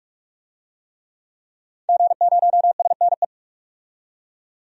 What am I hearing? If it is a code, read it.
Morse code, heard as a keyed tone, about 46 wpm, 700 Hz. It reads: G0SNE
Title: DXCC-100 Call Signs - 5 Characters - 45wpm